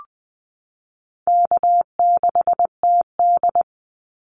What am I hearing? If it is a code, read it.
Morse code, heard as a keyed tone, about 20 wpm, 700 Hz. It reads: K6TD
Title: Easy Call Signs Repeated 2x - 20wpm